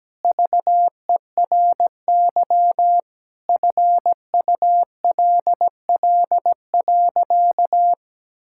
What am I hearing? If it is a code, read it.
Morse code, heard as a keyed tone, about 17 wpm, 700 Hz. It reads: VERY FULL.
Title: Sentences from Top 1600 Words 17wpm